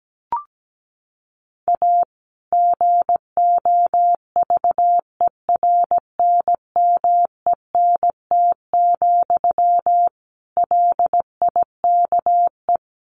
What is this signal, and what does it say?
Morse code, heard as a keyed tone, about 17 wpm, 700 Hz. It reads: A GOVERNMENT, LIKE
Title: Sentences from Top 1400 Words 17wpm